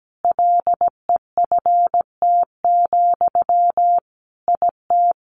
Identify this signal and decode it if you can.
Morse code, heard as a keyed tone, about 17 wpm, 700 Hz. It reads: LEFT, IT
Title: Sentences from Top 400 Words 17wpm